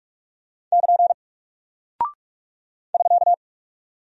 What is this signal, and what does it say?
Morse code, heard as a keyed tone, about 44 wpm, 700 Hz. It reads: <KN> <SK>
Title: Single Character (Letter:Number:Punctuation:Prosign) Alt 45wpm